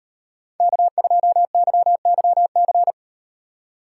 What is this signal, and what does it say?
Morse code, heard as a keyed tone, about 38 wpm, 700 Hz. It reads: K2YYC